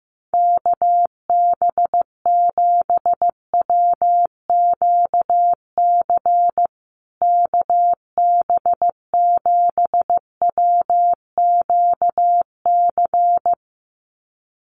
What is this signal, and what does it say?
Morse code, heard as a keyed tone, about 14 wpm, 700 Hz. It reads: KB7WQC KB7WQC
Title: USA Call Signs Repeated 2x 15wpm